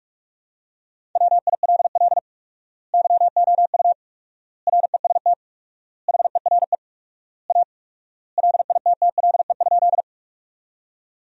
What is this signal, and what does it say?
Morse code, heard as a keyed tone, about 42 wpm, 690 Hz. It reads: WILL YOU REST HERE A LITTLE?
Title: 4x Word-Spacing - Sentences from Top 500 Words - 45wpm